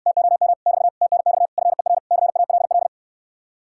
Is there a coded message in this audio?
Yes: Morse code from 0.1 to 2.9 s, heard as a keyed tone, about 69 wpm, 700 Hz. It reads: TOM <BT> TNX FER CALL